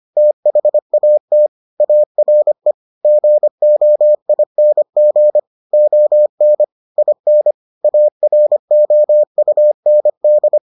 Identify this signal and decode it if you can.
Morse code, heard as a keyed tone, about 25 wpm, 600 Hz. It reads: THAT ARE GOING ON IN AROUND